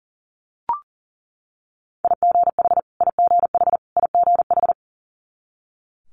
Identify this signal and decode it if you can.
Morse code, heard as a keyed tone, about 40 wpm, 700 Hz. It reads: IGH IGH IGH
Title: Multi-letter Phonemes Repeated 3x 40wpm